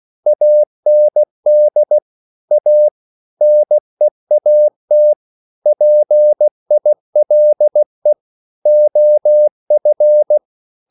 Morse code, heard as a keyed tone, about 16 wpm, 600 Hz. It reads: AND A NEAT PILE OF